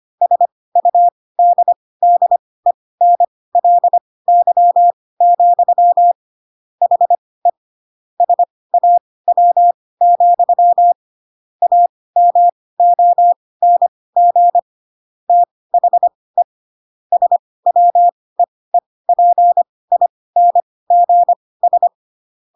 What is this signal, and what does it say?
Morse code, heard as a keyed tone, about 25 wpm, spaced as for 18 wpm, 700 Hz. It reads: SUDDENLY, HE SAW, AMONG THE SWEEPINGS
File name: Pinocchio / Pinocchio ch5 12 25 ik1hge